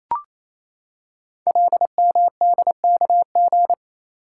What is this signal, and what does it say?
Morse code, heard as a keyed tone, about 27 wpm, 700 Hz. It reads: LMDKG
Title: Code Groups - 5 Letters - 28wpm